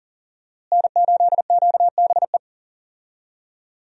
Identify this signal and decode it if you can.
Morse code, heard as a keyed tone, about 40 wpm, 700 Hz. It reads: N8QBE
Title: USA Call Signs - 5 Characters - Repeated 2x - 40wpm